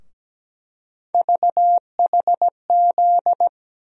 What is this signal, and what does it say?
Morse code, heard as a keyed tone, about 17 wpm, 700 Hz. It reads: VHZ
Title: USA Call Sign Postfixes - 3 Letters 17wpm